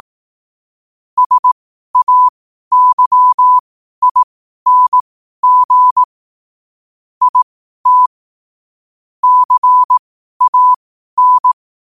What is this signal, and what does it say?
Morse code, heard as a keyed tone, about 18 wpm, spaced as for 12 wpm, 1000 Hz. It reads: SAYING IT CAN
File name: qotdfiles / 12wpm